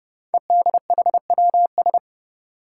Morse code, heard as a keyed tone, about 30 wpm, 700 Hz. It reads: EDHWS
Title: Five Challenging Letter-Number Combinations 30wpm